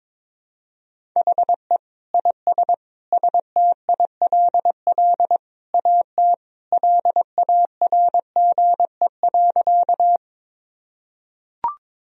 Morse code, heard as a keyed tone, about 22 wpm, 700 Hz. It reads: HE IS STILL AT LARGE.
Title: Sentences from Top 1000 Words - Review Part 1 - 22wpm